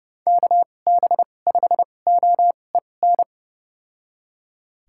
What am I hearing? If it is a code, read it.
Morse code, heard as a keyed tone, about 30 wpm, spaced as for 22 wpm, 700 Hz. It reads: KB5OEN
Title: USA Call Signs - 2x Character Spacing - 30wpm